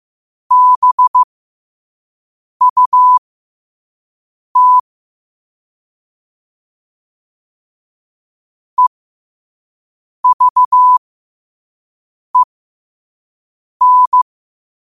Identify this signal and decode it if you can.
Morse code, heard as a keyed tone, about 15 wpm, spaced as for 5 wpm, 1000 Hz. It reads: BUT EVEN
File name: qotdfiles / 5wpm